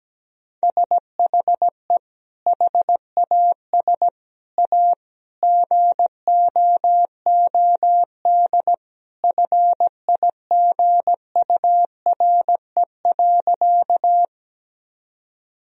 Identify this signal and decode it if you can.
Morse code, heard as a keyed tone, about 17 wpm, 700 Hz. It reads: SHE HAS A GOOD FIGURE.
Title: Sentences from Top 1000 Words - Review Part 2 - 17wpm